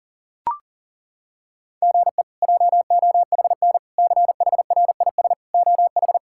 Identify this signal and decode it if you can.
Morse code, heard as a keyed tone, about 40 wpm, 700 Hz. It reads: GE JOHN CHRIS OH
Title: K1USN SST QSO Practice - 40wpm